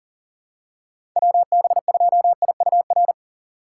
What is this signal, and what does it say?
Morse code, heard as a keyed tone, about 40 wpm, 700 Hz. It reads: WB2IUR
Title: USA Call Signs - 6 Characters - 4x Character Spacing - 40wpm